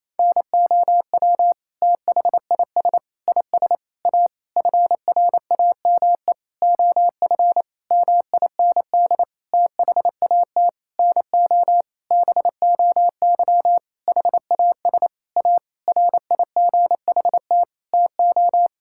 Morse code, heard as a keyed tone, about 27 wpm, 700 Hz. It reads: NOW THIS IS A FRAME OF MIND THAT NO BOY HAS A RIGHT TO
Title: Sentences from Top 1000 Words - Review Part 6 - 28wpm